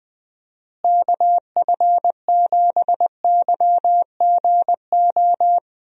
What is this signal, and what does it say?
Morse code, heard as a keyed tone, about 20 wpm, 700 Hz. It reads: KF7YGO
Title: USA Call Signs - 6 Characters - Repeated 2x - 20wpm